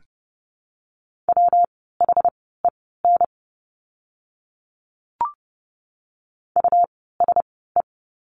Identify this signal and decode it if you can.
Morse code, heard as a keyed tone, about 30 wpm, spaced as for 17 wpm, 700 Hz. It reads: WHEN USE
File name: course / 30:10wpm / 115 Lesson 19f - Letter w - Words 30wpm